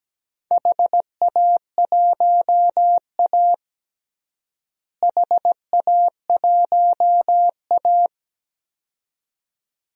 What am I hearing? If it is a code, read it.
Morse code, heard as a keyed tone, about 17 wpm, 700 Hz. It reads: HA1A HA1A
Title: DXCC-100 Call Signs - 4 Characters - Repeated 2x - 17wpm